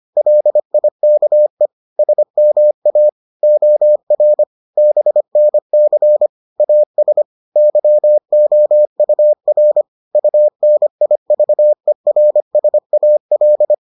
Morse code, heard as a keyed tone, about 25 wpm, 600 Hz. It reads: LIKE SMA OR BNC AS YOUR UNIVERSAL